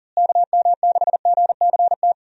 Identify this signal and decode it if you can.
Morse code, heard as a keyed tone, about 37 wpm, 700 Hz. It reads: KM6GCT
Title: USA Call Signs - 6 Characters - Repeated 2x - 40wpm